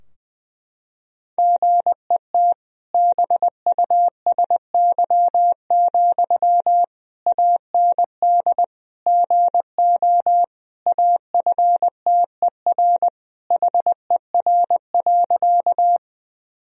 Morse code, heard as a keyed tone, about 19 wpm, 700 Hz. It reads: GET BUSY, AND GO AFTER HER.
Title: Sentences from Top 1000 Words - Review Part 3 - 20wpm